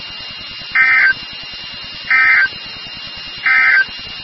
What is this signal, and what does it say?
Emergency Alert System (SAME) tones, heard as harsh buzzing bursts. It is heard from 0.6 to 3.9 s.
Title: NOAA Weather Radio WXK36